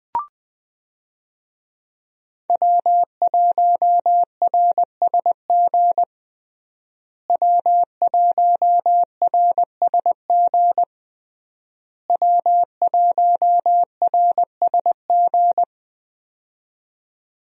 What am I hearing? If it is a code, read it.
Morse code, heard as a keyed tone, about 21 wpm, 700 Hz. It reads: W1RSG W1RSG W1RSG
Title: USA Call Signs - 5 Characters - Repeated 3x - 20wpm